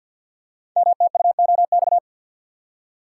Morse code, heard as a keyed tone, about 50 wpm, 700 Hz. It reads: MTUOX
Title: Code Groups - 5 Letters - 50wpm